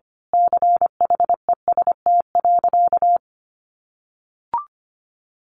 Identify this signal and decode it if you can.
Morse code, heard as a keyed tone, about 25 wpm, 700 Hz. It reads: CHEST.
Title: A Sound Of Thunder 25wpm